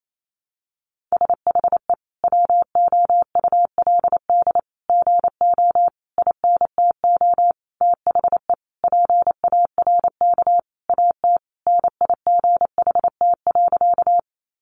Morse code, heard as a keyed tone, about 28 wpm, 700 Hz. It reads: SHE WOULD GO INTO THE PARK AT NIGHT.
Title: Sentences from Top 1300 Words 28wpm